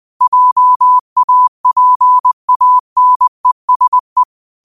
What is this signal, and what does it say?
Morse code, heard as a keyed tone, about 20 wpm, 1000 Hz. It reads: JAPANESE
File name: qotdfiles / 20wpm